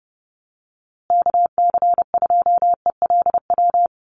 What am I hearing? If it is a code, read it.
Morse code, heard as a keyed tone, about 30 wpm, 700 Hz. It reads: KC2ELW
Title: USA Call Signs Repeated 2x 30wpm